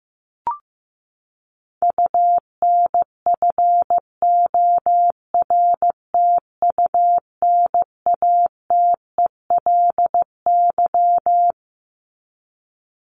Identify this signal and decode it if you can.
Morse code, heard as a keyed tone, about 15 wpm, 700 Hz. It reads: UNFORTUNATELY
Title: LY Suffix - Top 100 Words with Base Words 15wpm